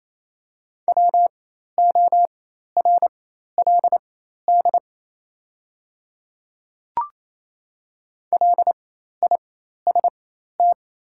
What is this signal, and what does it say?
Morse code, heard as a keyed tone, about 28 wpm, spaced as for 13 wpm, 700 Hz. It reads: WORLD LIST
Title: 4x Character Spacing - Top 500 Words - 28wpm